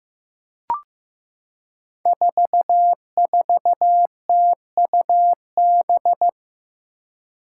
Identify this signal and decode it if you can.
Morse code, heard as a keyed tone, about 15 wpm, 700 Hz. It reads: 44TUB
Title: Five Challenging Letter-Number Combinations 15wpm